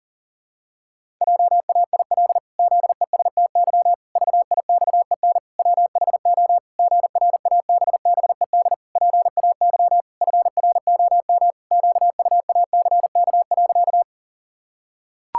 Morse code, heard as a keyed tone, about 40 wpm, 700 Hz. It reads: JAIL ZESTY VIXEN WHO GRABBED PAY FROM QUACK.
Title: Pangram Challenge v2 - 40wpm